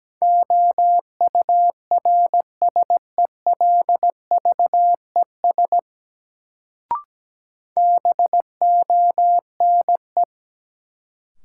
Morse code, heard as a keyed tone, about 17 wpm, 700 Hz. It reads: OURSELVES BONE
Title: Sentences from Top 1600 Words - New Word Drill - 17wpm